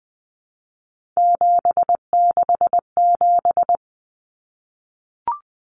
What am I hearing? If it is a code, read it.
Morse code, heard as a keyed tone, about 20 wpm, 700 Hz. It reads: 767
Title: Numbers - 3-Digit 20wpm